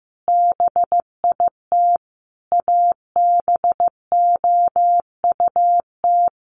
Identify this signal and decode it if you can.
Morse code, heard as a keyed tone, about 15 wpm, 700 Hz. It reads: BIT ABOUT
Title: Sets of 5 Words Encore 15wpm